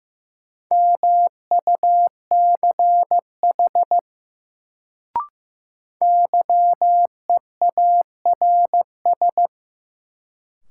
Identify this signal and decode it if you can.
Morse code, heard as a keyed tone, about 16 wpm, 700 Hz. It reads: MUCH YEARS
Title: Sentences from Top 200 Words - New Word Drill - 15wpm